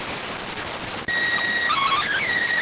SSTV (slow-scan television), PD 290.